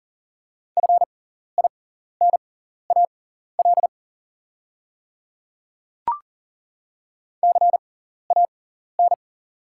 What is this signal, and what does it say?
Morse code, heard as a keyed tone, about 40 wpm, spaced as for 14 wpm, 690 Hz. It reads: FINAL CAN
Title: 6x Character Spacing - Top 500 Words - 40wpm